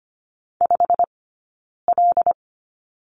Morse code, heard as a keyed tone, about 25 wpm, 700 Hz. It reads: 5 L
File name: Lesson 15d - Number 5 - Two Characters 25wpm